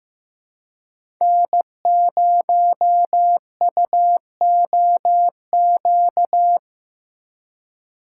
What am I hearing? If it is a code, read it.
Morse code, heard as a keyed tone, about 15 wpm, 700 Hz. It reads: N0UOQ
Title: USA Call Signs - 5 Characters - 2x Character Spacing - 15wpm